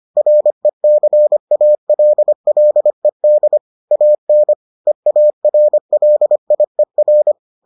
Morse code, heard as a keyed tone, about 25 wpm, 600 Hz. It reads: RECALLED AN EARLIER